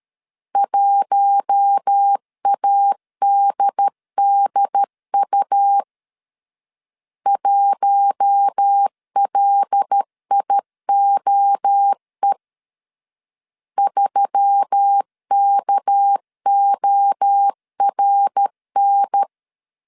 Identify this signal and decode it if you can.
Morse code, heard as a keyed tone, about 13 wpm, 800 Hz. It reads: 1ADDU 1LIOE 3KORN
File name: CW / DET / 1 6 1500C S A 12wpm 3440